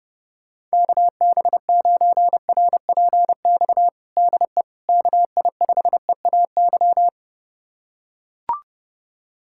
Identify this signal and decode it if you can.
Morse code, heard as a keyed tone, about 30 wpm, 700 Hz. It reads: KB9RPX DE KI5EAY
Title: USA Callsign DE Callsign - 6 Characters - 30wpm